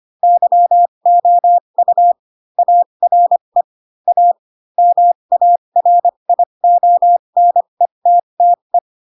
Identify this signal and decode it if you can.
Morse code, heard as a keyed tone, about 25 wpm, spaced as for 22 wpm, 700 Hz. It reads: YOU ARE A MARIONETTE
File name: Pinocchio / Pinocchio ch4 18 25 ik1hge